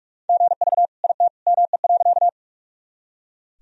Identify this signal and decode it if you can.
Morse code, heard as a keyed tone, about 44 wpm, 700 Hz. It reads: GV IT ME.
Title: Sentences from Top 200 Words and Abbreviations 45wpm